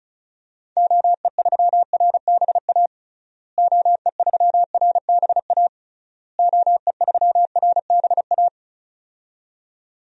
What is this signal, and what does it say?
Morse code, heard as a keyed tone, about 36 wpm, 700 Hz. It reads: OE3RBA OE3RBA OE3RBA
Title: DXCC-100 Call Signs - 6 Characters - Repeated 3x - 35wpm